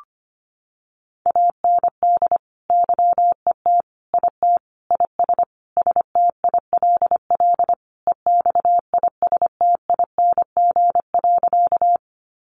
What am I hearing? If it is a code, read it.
Morse code, heard as a keyed tone, about 25 wpm, 700 Hz. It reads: AND YET IT IS STILL EXISTING.
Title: Sentences from Top 1500 Words 25wpm